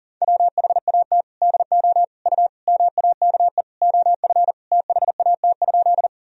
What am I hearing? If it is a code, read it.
Morse code, heard as a keyed tone, about 40 wpm, 700 Hz. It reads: WHAT DO U MAKE OF THAT?
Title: Sentences from Top 100 Words and Abbreviations 40wpm